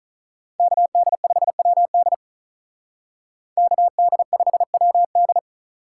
Morse code, heard as a keyed tone, about 46 wpm, 700 Hz. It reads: KD5WD KD5WD